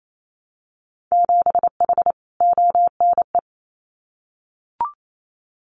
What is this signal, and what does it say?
Morse code, heard as a keyed tone, about 28 wpm, 700 Hz. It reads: 7H ONE
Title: Winter Field Day Exchanges Encore 28wpm